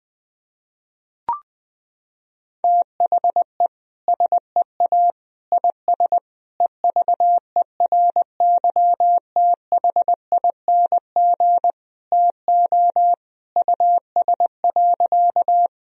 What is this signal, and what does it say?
Morse code, heard as a keyed tone, about 20 wpm, 700 Hz. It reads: THE SEA IS EVERYTHING TO US.
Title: Sentences from Top 1000 Words - Review Part 1 - 20wpm